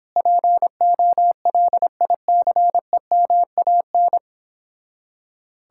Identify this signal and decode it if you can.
Morse code, heard as a keyed tone, about 29 wpm, 700 Hz. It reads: POLICEMAN